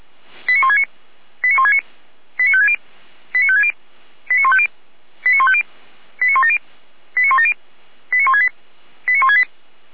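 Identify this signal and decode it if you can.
Five-tone selective call (ZVEI), heard as quick runs of five tones, tones of about 70 ms. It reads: 89169 89169 89560 89560 89150 89150 89170 89170 89168 89168
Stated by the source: ZVEI five-tone call received with an SDR, logged by the Feuerwehr Orlamünde audio server.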